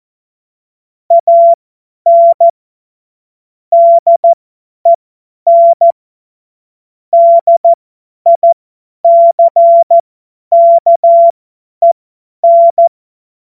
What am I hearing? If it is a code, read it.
Morse code, heard as a keyed tone, about 14 wpm, spaced as for 10 wpm, 680 Hz. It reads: AN DEN DICKEN
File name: Morsekurs / Witze / Witz 06 Februar 2024 50 F70